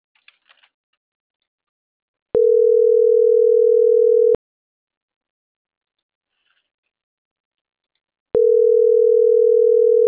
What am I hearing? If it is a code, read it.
Ringback tone (North America), 440 + 480 Hz.